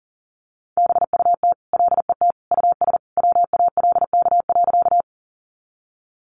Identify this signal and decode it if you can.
Morse code, heard as a keyed tone, about 40 wpm, 700 Hz. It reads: BUT LET US WALK.
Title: Sentences from Top 300 Words 40wpm